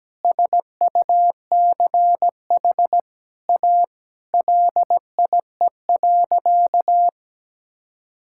Morse code, heard as a keyed tone, about 17 wpm, 700 Hz. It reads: SUCH A LIE.
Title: Sentences from Top 1600 Words 17wpm